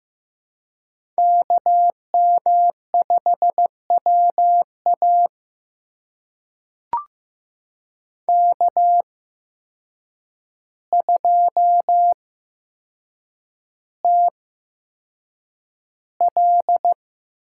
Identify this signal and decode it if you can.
Morse code, heard as a keyed tone, about 15 wpm, spaced as for 7 wpm, 700 Hz. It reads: KM5WA K2TL
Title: USA Call Signs - 8x Character Spacing - 15wpm